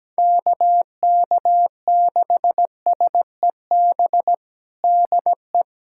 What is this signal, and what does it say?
Morse code, heard as a keyed tone, about 17 wpm, 700 Hz. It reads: KK6SEB DE
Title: USA Callsign DE Callsign - 6 Characters - 17wpm